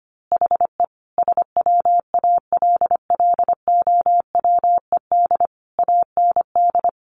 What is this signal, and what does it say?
Morse code, heard as a keyed tone, about 25 wpm, 700 Hz. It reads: HE SWALLOWED AND
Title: A Sound Of Thunder 25wpm